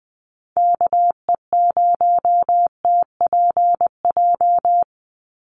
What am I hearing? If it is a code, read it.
Morse code, heard as a keyed tone, about 20 wpm, 700 Hz. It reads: KE0TPJ
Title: USA Call Signs 20wpm